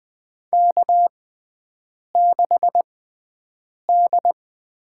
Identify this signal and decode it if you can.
Morse code, heard as a keyed tone, about 20 wpm, spaced as for 7 wpm, 700 Hz. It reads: K6D
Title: USA Call Signs - 6x Character Spacing - 20wpm